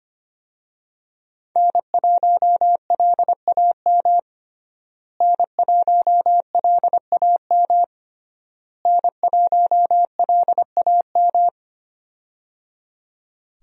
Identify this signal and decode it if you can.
Morse code, heard as a keyed tone, about 25 wpm, 700 Hz. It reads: N1LAM N1LAM N1LAM
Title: USA Call Signs - 5 Characters - Repeated 3x - 25wpm